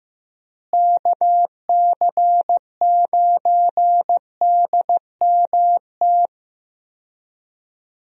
Morse code, heard as a keyed tone, about 16 wpm, 700 Hz. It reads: KC9DMT
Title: USA Call Signs - 6 Characters - Repeated 2x - 15wpm